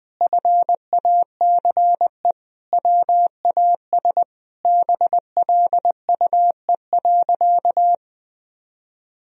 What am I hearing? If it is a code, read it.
Morse code, heard as a keyed tone, about 20 wpm, 700 Hz. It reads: FACE WAS BLUE.
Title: Sentences from Top 1000 Words - Review Part 3 - 20wpm